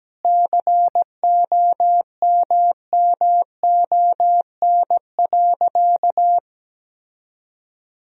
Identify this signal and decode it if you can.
Morse code, heard as a keyed tone, about 17 wpm, 700 Hz. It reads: COMMON.
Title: Sentences from Top 1000 Words - Review Part 3 - 17wpm